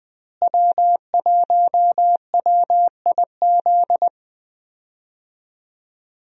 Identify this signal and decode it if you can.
Morse code, heard as a keyed tone, about 20 wpm, 700 Hz. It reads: W1WIZ
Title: USA Call Signs - 5 Characters - Repeated 3x - 20wpm